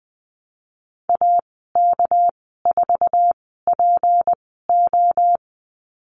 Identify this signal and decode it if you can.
Morse code, heard as a keyed tone, about 20 wpm, spaced as for 15 wpm, 700 Hz. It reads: AK4PO
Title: 207 Lesson 33g - Letter j - Call Signs 20wpm